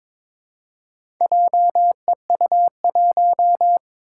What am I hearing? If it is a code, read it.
Morse code, heard as a keyed tone, about 22 wpm, 700 Hz. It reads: JEU1
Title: Four Challenging Letter-Number Combinations 22wpm